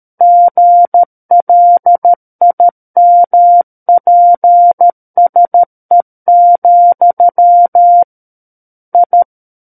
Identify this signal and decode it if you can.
Morse code, heard as a keyed tone, about 13 wpm, 700 Hz. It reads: GLIMPSE, I
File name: Book One / 04-Chapter 4 - 13 WPM